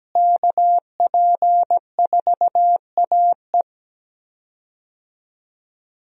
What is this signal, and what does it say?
Morse code, heard as a keyed tone, about 17 wpm, 700 Hz. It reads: KP4AE
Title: USA Call Signs - 5 Characters - Repeated 2x - 17wpm